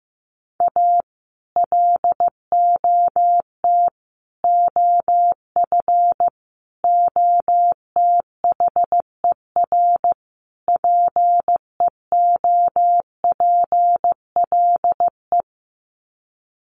Morse code, heard as a keyed tone, about 15 wpm, 700 Hz. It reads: A LOT OF OTHER PEOPLE
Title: Sets of 5 Words Encore2 15wpm